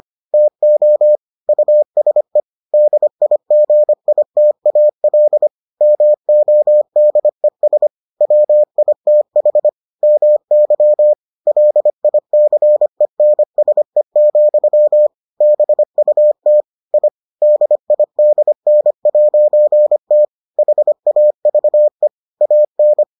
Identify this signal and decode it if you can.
Morse code, heard as a keyed tone, about 25 wpm, 600 Hz. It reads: TO USE DIGITAL MODES WITH MY LICENSE, BUT I DIDN'T HAVE AN